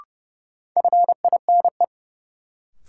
Morse code, heard as a keyed tone, about 30 wpm, 700 Hz. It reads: FINE
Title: Sentences from Top 500 Words - New Word Drill - 30wpm